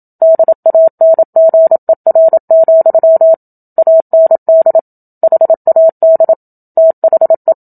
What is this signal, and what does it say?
Morse code, heard as a keyed tone, about 27 wpm, 650 Hz. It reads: DANGER, AND HAD THE
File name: Book Two / 01-Chapter 1 - 27 WPM